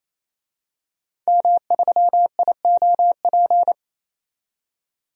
Morse code, heard as a keyed tone, about 27 wpm, 700 Hz. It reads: M3IOP
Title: DXCC-100 Call Signs - 5 Characters - 28wpm